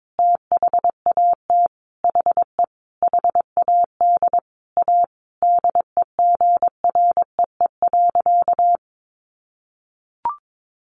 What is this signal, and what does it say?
Morse code, heard as a keyed tone, about 22 wpm, 700 Hz. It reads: THAT HE HAD A DEGREE.
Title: Sentences from Top 1500 Words 22wpm